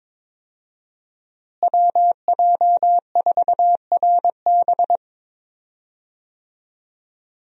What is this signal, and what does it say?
Morse code, heard as a keyed tone, about 22 wpm, 700 Hz. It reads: WJ4RB
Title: USA Call Signs - 5 Characters - Repeated 3x - 22wpm